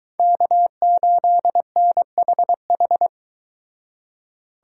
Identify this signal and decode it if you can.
Morse code, heard as a keyed tone, about 23 wpm, 700 Hz. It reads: K8NHH